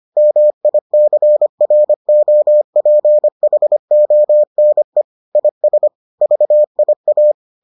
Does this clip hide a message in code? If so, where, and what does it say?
Yes: Morse code from 0.2 to 7.3 s, heard as a keyed tone, about 25 wpm, 600 Hz. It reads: MICROPHONE IS VIA